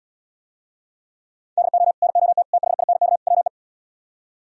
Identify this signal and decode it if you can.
Morse code, heard as a keyed tone, about 77 wpm, 700 Hz. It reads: DO NOT THINK WE